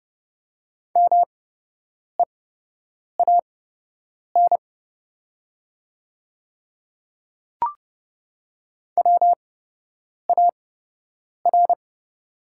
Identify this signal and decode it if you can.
Morse code, heard as a keyed tone, about 30 wpm, spaced as for 8 wpm, 700 Hz. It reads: MEAN WAR
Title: 8x Character Spacing - Top 500 Words - 30wpm